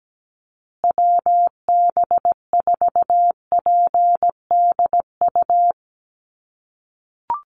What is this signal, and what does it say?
Morse code, heard as a keyed tone, about 17 wpm, 700 Hz. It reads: WB4PDU
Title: USA Call Signs Repeated 2x 17wpm